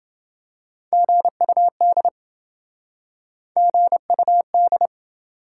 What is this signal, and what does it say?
Morse code, heard as a keyed tone, about 27 wpm, 700 Hz. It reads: GUD GUD